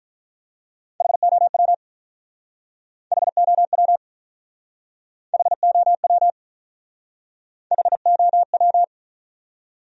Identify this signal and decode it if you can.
Morse code, heard as a keyed tone, about 45 wpm, 700 Hz. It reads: HOW HOW HOW HOW